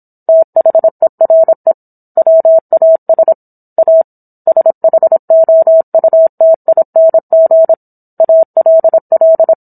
Morse code, heard as a keyed tone, about 27 wpm, 650 Hz. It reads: THERE WAS A SHOUTING ALL
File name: Book One / 17-Chapter 17 - 26 WPM